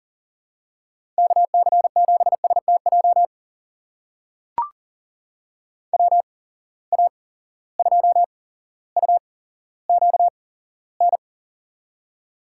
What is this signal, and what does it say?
Morse code, heard as a keyed tone, about 40 wpm, spaced as for 17 wpm, 700 Hz. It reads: KC7STJ WA2UQN
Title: USA Call Signs - 6 Characters - 8x Character Spacing - 40wpm